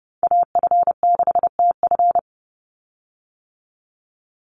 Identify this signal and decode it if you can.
Morse code, heard as a keyed tone, about 30 wpm, 700 Hz. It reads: AF6TF
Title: USA Call Signs 30wpm